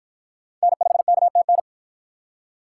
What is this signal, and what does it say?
Morse code, heard as a keyed tone, about 46 wpm, 700 Hz. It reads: N5KTN